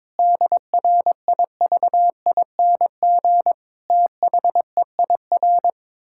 Morse code, heard as a keyed tone, about 22 wpm, 700 Hz. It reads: DRIVING THEIR
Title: Pangram Challenge v2 - 22wpm